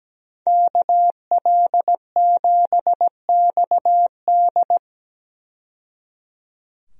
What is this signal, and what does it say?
Morse code, heard as a keyed tone, about 17 wpm, 700 Hz. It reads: KL7XD
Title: DXCC-100 Call Signs - 5 Characters - 17wpm